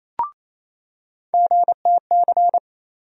Morse code, heard as a keyed tone, about 28 wpm, 700 Hz. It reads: GTC
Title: USA Call Sign Postfixes - 3 Letters 28wpm